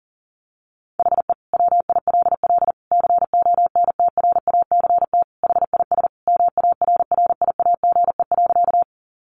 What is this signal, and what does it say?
Morse code, heard as a keyed tone, about 40 wpm, 700 Hz. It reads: HE WILL CONTRACT HIS MARRIAGE.
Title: Sentences from Top 1400 Words 40wpm